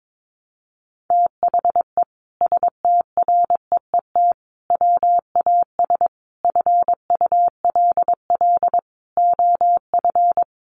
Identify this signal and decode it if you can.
Morse code, heard as a keyed tone, about 22 wpm, 700 Hz. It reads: THE STREET WAS FULL OF
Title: Sentences from Top 1300 Words 22wpm